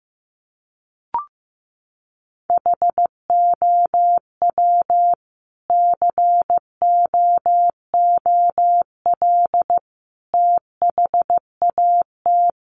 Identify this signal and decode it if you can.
Morse code, heard as a keyed tone, about 15 wpm, 700 Hz. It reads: HOW COOL THAT
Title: Sentences from Top 1200 Words 15wpm